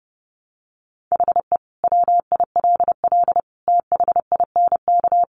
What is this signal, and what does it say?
Morse code, heard as a keyed tone, about 30 wpm, 700 Hz. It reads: HE WILL THINK
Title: Sentences from Top 1100 Words 30wpm